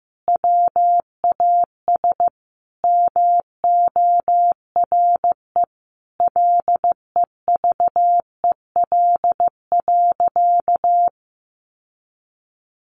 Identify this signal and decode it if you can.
Morse code, heard as a keyed tone, about 15 wpm, 700 Hz. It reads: WAS MORE LEVEL.
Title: Sentences from Top 700 Words 15wpm